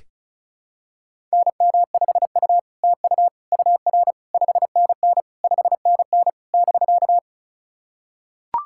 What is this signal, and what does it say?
Morse code, heard as a keyed tone, about 36 wpm, 700 Hz. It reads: NM5U TU UR 5NN 5NN <BK>
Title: SOTA QSO Practice 35wpm